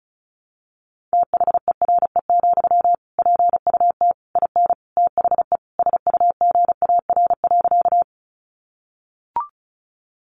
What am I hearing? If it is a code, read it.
Morse code, heard as a keyed tone, about 34 wpm, 700 Hz. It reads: THERE, PUT IN THE SUGAR.
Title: Sentences from Top 1200 Words 35wpm